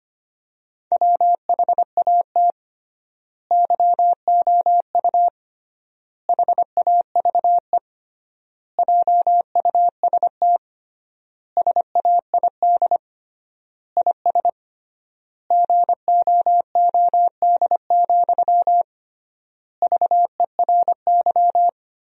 Morse code, heard as a keyed tone, about 25 wpm, 700 Hz. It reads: WHAT YOU HAVE JUST SAID IS GOOD, VERY
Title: 3x Word-Spacing - Sentences from Top 200 Words - 25wpm